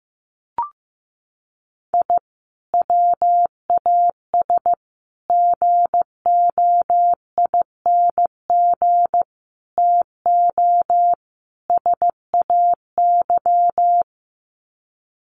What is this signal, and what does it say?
Morse code, heard as a keyed tone, about 15 wpm, 700 Hz. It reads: I WAS GOING TO SAY
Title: Sets of 5 Words 15wpm